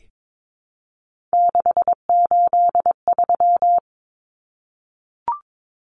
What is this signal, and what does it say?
Morse code, heard as a keyed tone, about 22 wpm, 700 Hz. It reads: 683
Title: Numbers - 3-Digit 22wpm